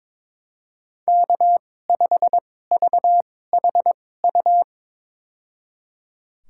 Morse code, heard as a keyed tone, about 21 wpm, spaced as for 16 wpm, 700 Hz. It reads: K5VHU
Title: USA Call Signs - 5 Characters - 2x Character Spacing - 22wpm